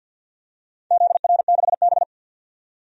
Morse code, heard as a keyed tone, about 51 wpm, 700 Hz. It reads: ZR6B
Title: DXCC-100 Call Signs - 4 Characters - 50wpm